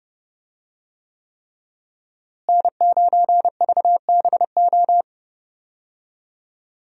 Morse code, heard as a keyed tone, about 30 wpm, 700 Hz. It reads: N9VBO
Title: USA Call Signs - 5 Characters - Repeated 3x - 30wpm